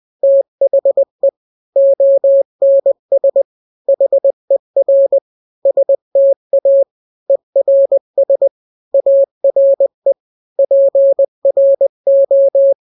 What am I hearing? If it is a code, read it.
Morse code, heard as a keyed tone, about 20 wpm, 550 Hz. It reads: THE ONS HER STA ERS ARE PRO
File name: n-grams / 3-grams 20 18 WPM